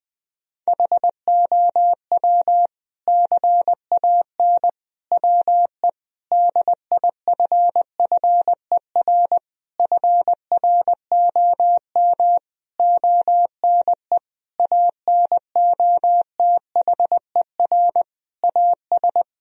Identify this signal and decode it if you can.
Morse code, heard as a keyed tone, about 20 wpm, 700 Hz. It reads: HOW CAN WE DIFFER FROM ONE ANOTHER AS
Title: Sentences from Top 1000 Words - Review Part 1 - 20wpm